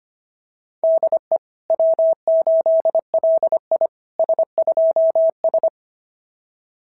Morse code, heard as a keyed tone, about 25 wpm, 660 Hz. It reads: DE W8LI S2S
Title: SOTA QSO Practice 25wpm